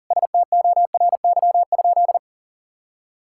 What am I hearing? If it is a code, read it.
Morse code, heard as a keyed tone, about 40 wpm, 700 Hz. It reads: STORY?
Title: Sentences from Top 1000 Words - Review Part 5 - 40wpm